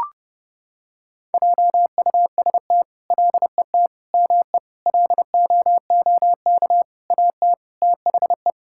Morse code, heard as a keyed tone, about 30 wpm, 700 Hz. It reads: JUST LET ME LOOK AT THE
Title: Sentences from Top 1000 Words - Review Part 5 - 30wpm